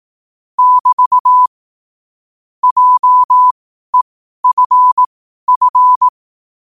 Morse code, heard as a keyed tone, about 18 wpm, spaced as for 12 wpm, 1000 Hz. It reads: <BT> JEFF
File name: qotdfiles / 12wpm